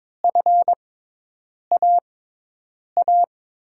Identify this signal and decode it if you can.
Morse code, heard as a keyed tone, about 21 wpm, spaced as for 8 wpm, 700 Hz. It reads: FAA
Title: USA Call Signs - 6 Characters - 6x Character Spacing - 22wpm